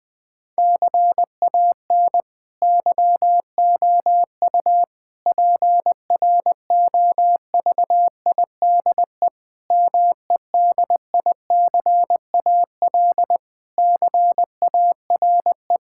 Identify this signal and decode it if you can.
Morse code, heard as a keyed tone, about 20 wpm, 700 Hz. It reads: CAN YOU PROVIDE MEDICAL CARE
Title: Sentences from Top 1000 Words - Review Part 5 - 20wpm